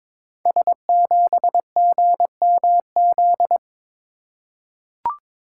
Morse code, heard as a keyed tone, about 23 wpm, 700 Hz. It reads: S7GMZ
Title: Code Groups - 5 Letter:Numbers - 22wpm